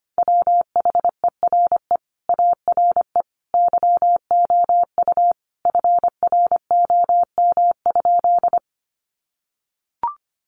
Morse code, heard as a keyed tone, about 25 wpm, 700 Hz. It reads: WHERE ARE YOU FROM?
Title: Sentences from Top 200 Words 25wpm